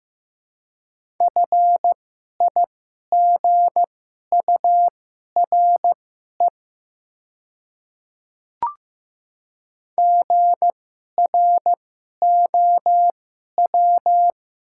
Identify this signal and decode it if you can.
Morse code, heard as a keyed tone, about 15 wpm, spaced as for 11 wpm, 700 Hz. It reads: FIGURE GROW
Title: 2x Character Spacing - Top 500 Words - 15wpm